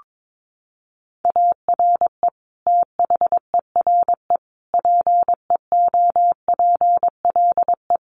Morse code, heard as a keyed tone, about 22 wpm, 700 Hz. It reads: ARE THERE PEOPLE
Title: Sentences from Top 1400 Words 22wpm